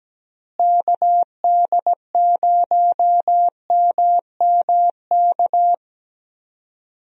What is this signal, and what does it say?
Morse code, heard as a keyed tone, about 17 wpm, 700 Hz. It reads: KD0MMK
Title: USA Call Signs - 6 Characters - Repeated 3x - 17wpm